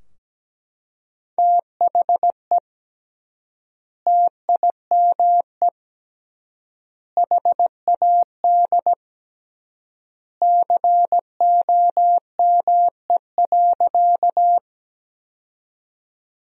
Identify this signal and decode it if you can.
Morse code, heard as a keyed tone, about 17 wpm, 700 Hz. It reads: THE TIME HAD COME.
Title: 3x Word-Spacing - Sentences from Top 100 Words - 17wpm